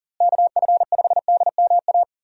Morse code, heard as a keyed tone, about 40 wpm, 700 Hz. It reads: KF5DMA
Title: USA Call Signs - 6 Characters - Repeated 3x - 40wpm